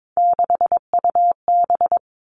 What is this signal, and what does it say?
Morse code, heard as a keyed tone, about 22 wpm, 700 Hz. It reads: 6UB
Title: All Cut Numbers - 3-Digit 22wpm